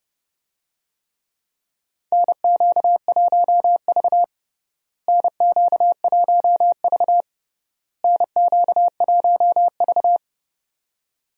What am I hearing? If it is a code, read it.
Morse code, heard as a keyed tone, about 32 wpm, 700 Hz. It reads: NQ1V NQ1V NQ1V
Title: USA Call Signs - 4 Characters - Repeated 3x - 30wpm